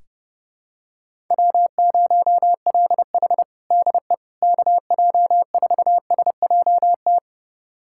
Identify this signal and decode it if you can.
Morse code, heard as a keyed tone, about 30 wpm, 700 Hz. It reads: W0LH DE KJ4SJT
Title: USA Callsign DE Callsign 30wpm